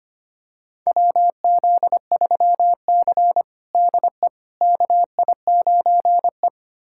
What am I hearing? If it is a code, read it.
Morse code, heard as a keyed tone, about 25 wpm, 700 Hz. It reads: WZ3C DE KI9E
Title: USA Callsign DE Callsign - 4 Characters - 25wpm